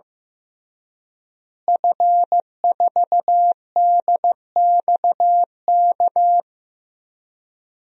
Morse code, heard as a keyed tone, about 15 wpm, 700 Hz. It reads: F4DXK
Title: DXCC-100 Call Signs - 5 Characters - Repeated 2x - 15wpm